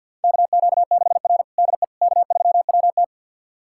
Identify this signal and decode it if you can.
Morse code, heard as a keyed tone, about 53 wpm, 690 Hz. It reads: KQ6R DE K3WT
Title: USA Callsign DE Callsign - 4 Characters - 50wpm